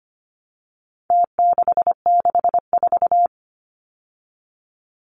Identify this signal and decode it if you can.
Morse code, heard as a keyed tone, about 25 wpm, 700 Hz. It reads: T664
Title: Numbers - Contest Style - 4-Digit 25wpm